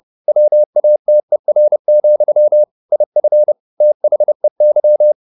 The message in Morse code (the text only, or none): WATER, IF THEY